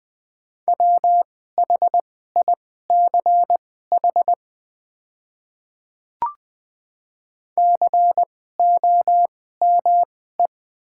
Morse code, heard as a keyed tone, about 20 wpm, spaced as for 14 wpm, 690 Hz. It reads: WHICH COME
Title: 2x Character Spacing - Top 100 Words - 20wpm